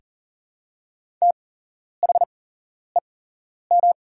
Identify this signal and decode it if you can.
Morse code, heard as a keyed tone, about 38 wpm, spaced as for 11 wpm, 700 Hz. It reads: THEM
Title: 8x Character Spacing - Top 500 Words - 40wpm